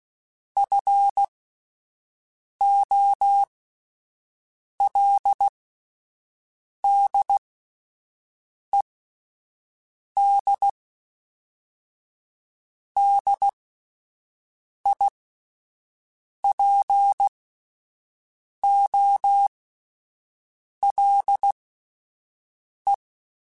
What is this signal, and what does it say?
Morse code, heard as a keyed tone, about 16 wpm, spaced as for 5 wpm, 800 Hz. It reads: FOLDED DIPOLE